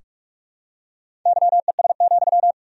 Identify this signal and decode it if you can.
Morse code, heard as a keyed tone, about 46 wpm, 700 Hz. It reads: YES,
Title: 3x Word-Spacing - Sentences from Top 500 Words - 45wpm